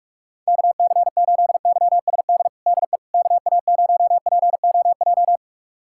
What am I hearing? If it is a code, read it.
Morse code, heard as a keyed tone, about 46 wpm, 700 Hz. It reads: KC8YSD DE KA0POJ
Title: USA Callsign DE Callsign - 6 Characters - 45wpm